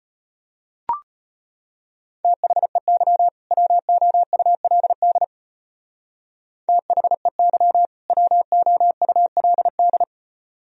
Morse code, heard as a keyed tone, about 33 wpm, 700 Hz. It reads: THEY WOULD THEY WOULD